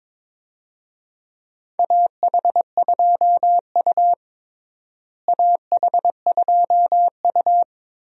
Morse code, heard as a keyed tone, about 22 wpm, 700 Hz. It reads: AH2U AH2U
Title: USA Call Signs - 4 Characters - Repeated 3x - 22wpm